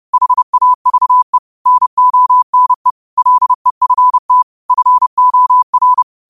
Morse code, heard as a keyed tone, about 30 wpm, 1000 Hz. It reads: HAVE NONE LEFT FOR
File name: qotdfiles / 30wpm